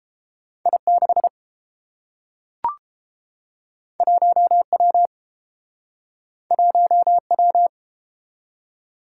Morse code, heard as a keyed tone, about 30 wpm, 700 Hz. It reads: I6 1W 1W